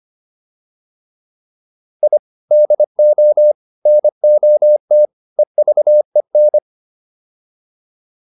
Morse code, heard as a keyed tone, about 25 wpm, 600 Hz. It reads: I DO NOT EVEN